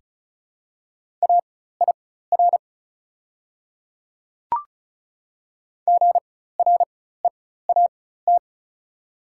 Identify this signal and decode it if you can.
Morse code, heard as a keyed tone, about 36 wpm, spaced as for 16 wpm, 690 Hz. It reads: AIR GREAT
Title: 4x Character Spacing - Top 300 Words - 35wpm